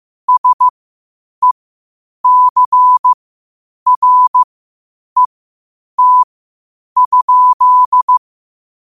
Morse code, heard as a keyed tone, about 15 wpm, spaced as for 8 wpm, 1000 Hz. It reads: SECRET?
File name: qotdfiles / 8wpm